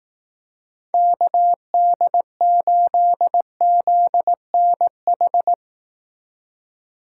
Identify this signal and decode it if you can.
Morse code, heard as a keyed tone, about 19 wpm, 700 Hz. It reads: KD8ZNH